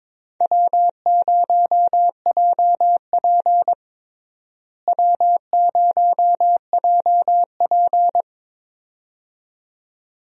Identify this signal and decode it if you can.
Morse code, heard as a keyed tone, about 21 wpm, 700 Hz. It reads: W0JP W0JP
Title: USA Call Signs - 4 Characters - Repeated 2x - 22wpm